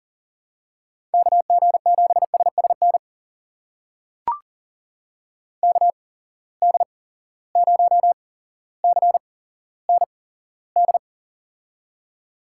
Morse code, heard as a keyed tone, about 40 wpm, spaced as for 17 wpm, 700 Hz. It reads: KG7SSN KD0CND
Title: USA Call Signs - 6 Characters - 8x Character Spacing - 40wpm